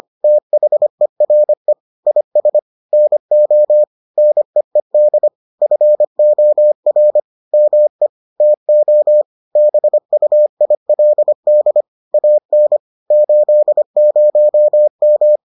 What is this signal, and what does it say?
Morse code, heard as a keyed tone, about 25 wpm, 600 Hz. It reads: THERE IS NO NEED FOR ME TO BUILD AN 80M